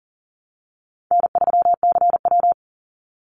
Morse code, heard as a keyed tone, about 39 wpm, 700 Hz. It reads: N3CW
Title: Easy Call Signs Repeated 3x - 40wpm